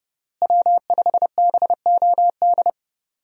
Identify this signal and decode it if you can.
Morse code, heard as a keyed tone, about 32 wpm, 700 Hz. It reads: W5BOD
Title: USA Call Signs - 5 Characters - Repeated 3x - 30wpm